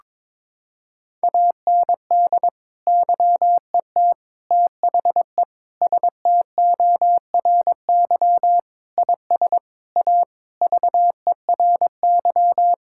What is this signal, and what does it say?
Morse code, heard as a keyed tone, about 22 wpm, 700 Hz. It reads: AND YET THE STORY IS A VERY
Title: Sentences from Top 1000 Words - Review Part 6 - 22wpm